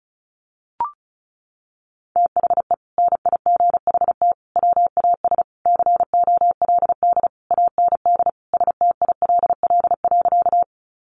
Morse code, heard as a keyed tone, about 34 wpm, 700 Hz. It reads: THE NIGHT WAS COLD AND STILL.
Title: Sentences from Top 400 Words 35wpm